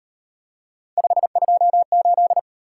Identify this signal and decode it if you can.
Morse code, heard as a keyed tone, about 37 wpm, 700 Hz. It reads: 528